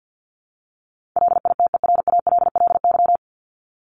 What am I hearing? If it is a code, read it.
Morse code, heard as a keyed tone, about 50 wpm, 700 Hz. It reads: LITERALLY
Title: Top 1251-1500 Words 50wpm